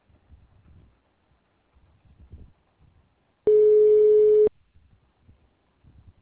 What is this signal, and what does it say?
Ringback tone (Europe, 425 Hz).